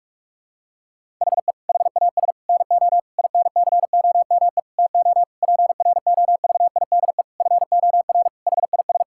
Morse code, heard as a keyed tone, about 46 wpm, 700 Hz. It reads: HE HAS NO INCOME TO PROVIDE FOR HIS
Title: Sentences from Top 1000 Words - Review Part 5 - 45wpm